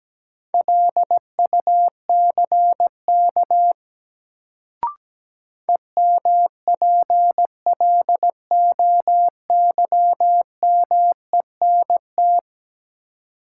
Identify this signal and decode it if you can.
Morse code, heard as a keyed tone, about 17 wpm, 700 Hz. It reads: LUCK EMPLOYMENT
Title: Sentences from Top 1500 Words - New Word Drill Alt - 17wpm